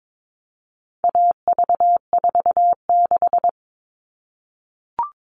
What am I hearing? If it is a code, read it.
Morse code, heard as a keyed tone, about 22 wpm, 700 Hz. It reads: AV46
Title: All Cut Numbers - 4-Digit 22wpm